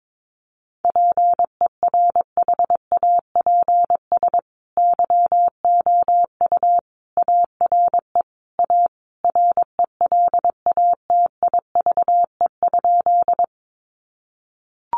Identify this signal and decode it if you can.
Morse code, heard as a keyed tone, about 22 wpm, 700 Hz. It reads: PERHAPS YOU ARE A RELATIVE?
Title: Sentences from Top 1400 Words 22wpm